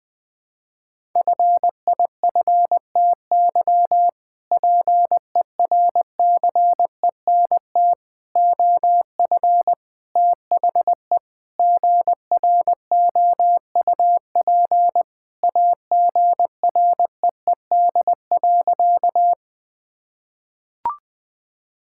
Morse code, heard as a keyed tone, about 20 wpm, 700 Hz. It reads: FIFTY PERCENT OF THE GROUP AGREED.
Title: Sentences with 3-Word Phrases - 20wpm